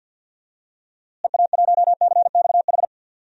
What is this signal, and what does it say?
Morse code, heard as a keyed tone, about 52 wpm, 700 Hz. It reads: EA1CXH
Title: DXCC-100 Call Signs - 6 Characters - Repeated 3x - 50wpm